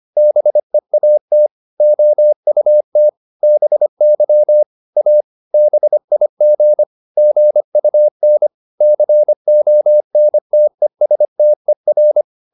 Morse code, heard as a keyed tone, about 25 wpm, 600 Hz. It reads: BEAT OUT BY A BIG GUN CONTESTER